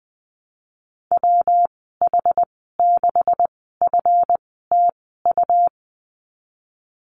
Morse code, heard as a keyed tone, about 20 wpm, spaced as for 14 wpm, 700 Hz. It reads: WH6FTU
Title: 179 Lesson 29g - Slash - Call Signs 20wpm